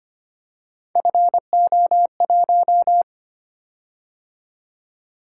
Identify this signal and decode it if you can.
Morse code, heard as a keyed tone, about 25 wpm, 700 Hz. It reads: FO1
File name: Three Letter-Number Combinations - 25wpm